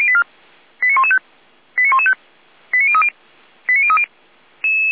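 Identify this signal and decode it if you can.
Five-tone selective call (ZVEI), heard as quick runs of five tones, tones of about 70 ms. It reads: 89115 89115 89030 89030
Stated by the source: Five-tone ZVEI selective call received by SDR.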